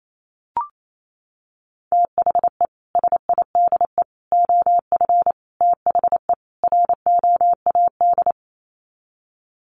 Morse code, heard as a keyed tone, about 28 wpm, 700 Hz. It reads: THE SIDE OF THE ROAD
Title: Sets of 5 Words 28wpm